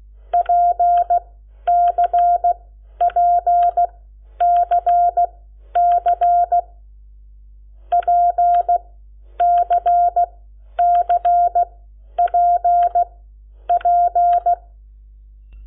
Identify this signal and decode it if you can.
Morse code, heard as a keyed tone, about 16 wpm, spaced as for 10 wpm, 700 Hz. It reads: PCPCC PCCPP
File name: Lecciones / CLASE Nº7